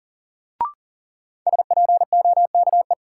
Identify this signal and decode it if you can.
Morse code, heard as a keyed tone, about 40 wpm, 700 Hz. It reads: SPOKE
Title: Sentences from Top 1200 Words - New Word Drill - 40wpm